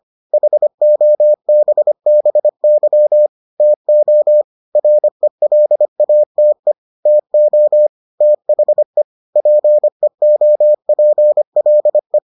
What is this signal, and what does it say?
Morse code, heard as a keyed tone, about 25 wpm, 600 Hz. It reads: HOBBY TO RELATE TO THE PEOPLE